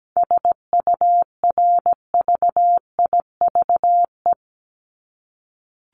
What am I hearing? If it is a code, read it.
Morse code, heard as a keyed tone, about 17 wpm, 700 Hz. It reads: SURVIVE
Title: Top 1251-1500 Words 17wpm